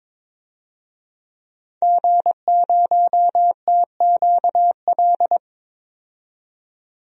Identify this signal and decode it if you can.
Morse code, heard as a keyed tone, about 22 wpm, 700 Hz. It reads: G0TQL
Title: DXCC-100 Call Signs - 5 Characters - 22wpm